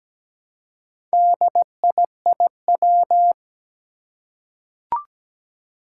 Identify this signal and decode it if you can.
Morse code, heard as a keyed tone, about 17 wpm, 700 Hz. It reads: DIIW
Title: Four Challenging Letter-Number Combinations 17wpm